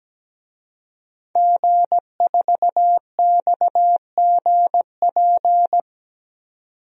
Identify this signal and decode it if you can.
Morse code, heard as a keyed tone, about 17 wpm, 700 Hz. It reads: G4XGP
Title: DXCC-100 Call Signs - 5 Characters - 17wpm